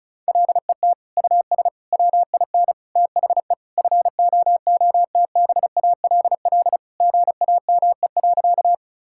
Morse code, heard as a keyed tone, about 34 wpm, 700 Hz. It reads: LET US WIN THE FOOTBALL GAME.
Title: Sentences from Top 1600 Words 35wpm